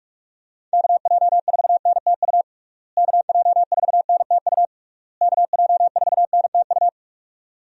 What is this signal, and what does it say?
Morse code, heard as a keyed tone, about 43 wpm, 700 Hz. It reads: KJ4NTU KJ4NTU KJ4NTU
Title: USA Call Signs - 6 Characters - Repeated 3x - 45wpm